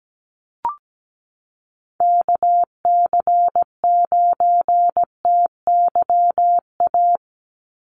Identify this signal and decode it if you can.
Morse code, heard as a keyed tone, about 17 wpm, 700 Hz. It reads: KC9TYA
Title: NA SOTA Chaser Callsigns 17wpm